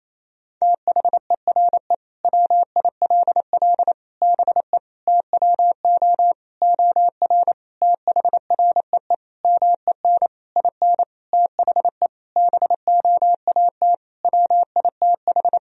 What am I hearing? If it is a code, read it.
Morse code, heard as a keyed tone, about 28 wpm, 700 Hz. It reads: THERE WILL BE TWO OR THREE MEN IN THE BOAT WITH
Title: Sentences from Top 1000 Words - Review Part 3 - 28wpm